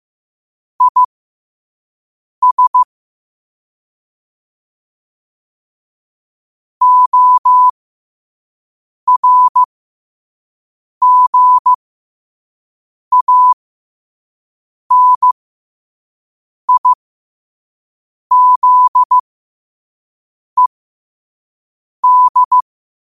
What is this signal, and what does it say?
Morse code, heard as a keyed tone, about 15 wpm, spaced as for 5 wpm, 1000 Hz. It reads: IS ORGANIZED
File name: qotdfiles / 5wpm